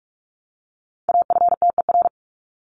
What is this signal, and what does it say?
Morse code, heard as a keyed tone, about 44 wpm, 700 Hz. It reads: AFTER
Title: Top 300 Words Repeated 45wpm